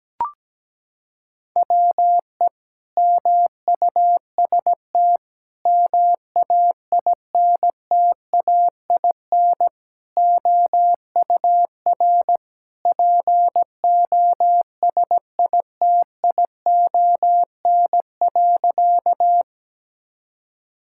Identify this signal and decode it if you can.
Morse code, heard as a keyed tone, about 17 wpm, 700 Hz. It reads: WE MUST MAINTAIN OUR POSITION.
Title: Sentences from Top 1000 Words - Review Part 5 - 17wpm